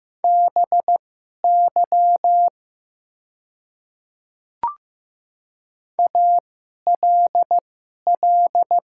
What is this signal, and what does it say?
Morse code, heard as a keyed tone, about 15 wpm, spaced as for 11 wpm, 700 Hz. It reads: BY ALL
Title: 2x Character Spacing - Top 100 Words - 15wpm